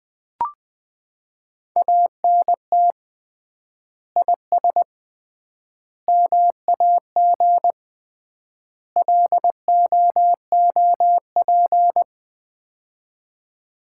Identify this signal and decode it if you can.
Morse code, heard as a keyed tone, about 20 wpm, 700 Hz. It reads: ANT IS MAG LOOP
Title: Common QSO Phrases - 3x Word-Spacing - 20wpm